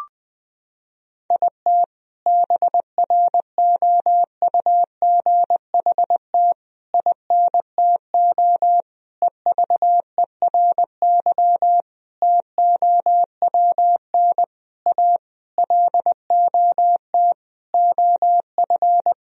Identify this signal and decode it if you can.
Morse code, heard as a keyed tone, about 20 wpm, 700 Hz. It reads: IT BROUGHT INTO EVERY TOWN A LOT OF
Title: Sentences from Top 1000 Words - Review Part 3 - 20wpm